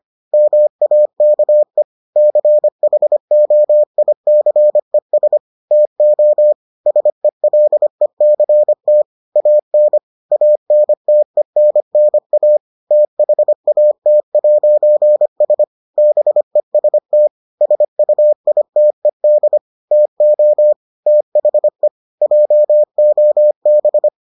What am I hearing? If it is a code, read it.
Morse code, heard as a keyed tone, about 25 wpm, 600 Hz. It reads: MAKE CHOICES TO SELECT AN ANTENNA THAT'S BEST SUITED TO THE JOB